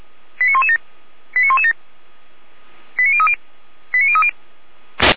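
Five-tone selective call (ZVEI), heard as quick runs of five tones, tones of about 80 ms. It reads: 89117 89117 89030 89030
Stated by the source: ZVEI five-tone selective call received on SDR.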